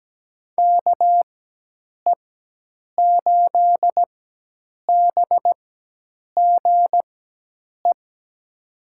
Morse code, heard as a keyed tone, about 17 wpm, spaced as for 8 wpm, 700 Hz. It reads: KE8BGE
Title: USA Call Signs - 6 Characters - 4x Character Spacing - 17wpm